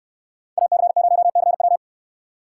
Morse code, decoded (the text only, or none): AC0CW